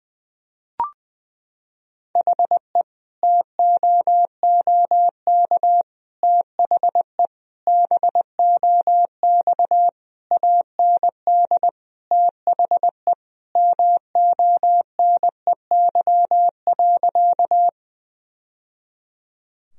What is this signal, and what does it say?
Morse code, heard as a keyed tone, about 20 wpm, 700 Hz. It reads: HE TOOK THE BOX AND THE MONEY.
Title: Sentences from Top 1000 Words - Review Part 2 - 20wpm